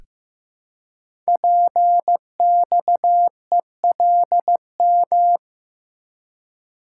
Morse code, heard as a keyed tone, about 15 wpm, 700 Hz. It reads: PXELM
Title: Code Groups - 5 Letters - 15wpm